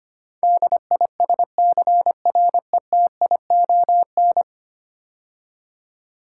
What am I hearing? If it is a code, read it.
Morse code, heard as a keyed tone, about 25 wpm, 700 Hz. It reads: DISCRETION
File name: 10 Letter Words - First 3 Letters Given - 25wpm